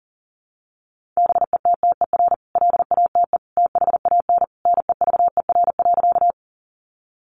Morse code, heard as a keyed tone, about 40 wpm, 700 Hz. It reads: BETTER LATE THAN NEVER.
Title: Sentences from Top 400 Words 40wpm